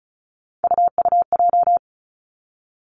Morse code, heard as a keyed tone, about 36 wpm, 700 Hz. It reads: UUJ
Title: Three Challenging Letter-Number Combinations 35wpm